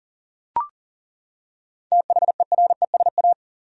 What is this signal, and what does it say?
Morse code, heard as a keyed tone, about 40 wpm, 700 Hz. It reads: THERESA
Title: USA Common Names - 40wpm